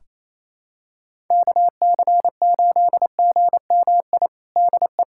Morse code, heard as a keyed tone, about 28 wpm, 700 Hz. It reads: KC8GMI DE
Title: USA Callsign DE Callsign - 6 Characters - 28wpm